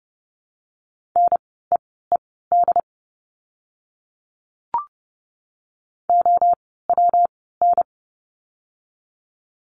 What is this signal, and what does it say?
Morse code, heard as a keyed tone, about 30 wpm, spaced as for 17 wpm, 700 Hz. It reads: NEED OWN
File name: course / 30:10wpm / 115 Lesson 19f - Letter w - Words 30wpm